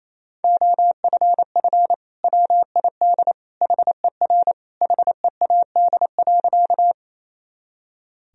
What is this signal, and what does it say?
Morse code, heard as a keyed tone, about 28 wpm, 700 Hz. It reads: OFF WID HER HEAD.
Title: Sentences from Top 300 Words and Abbreviations 28wpm